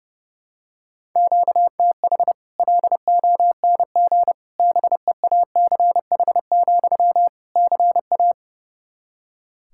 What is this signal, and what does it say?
Morse code, heard as a keyed tone, about 32 wpm, 700 Hz. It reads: QTH LONG BEACH, CA
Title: Practice QSO - Exchange 1 - Beginning - 30wpm